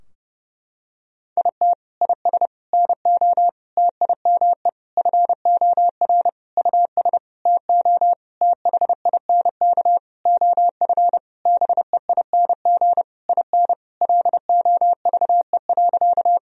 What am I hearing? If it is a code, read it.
Morse code, heard as a keyed tone, about 30 wpm, 700 Hz. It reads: IT IS NO TIME FOR US TO THINK OF BEING IN LOVE.
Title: Sentences from Top 1000 Words - Review Part 3 - 30wpm